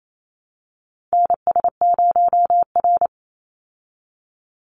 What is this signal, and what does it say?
Morse code, heard as a keyed tone, about 28 wpm, 700 Hz. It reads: NS0R
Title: Call Signs Repeated 2x - 28wpm